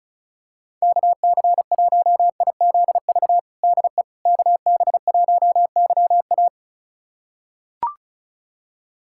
Morse code, heard as a keyed tone, about 36 wpm, 700 Hz. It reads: KC1IZV DE KB1YA
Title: USA Callsign DE Callsign 35wpm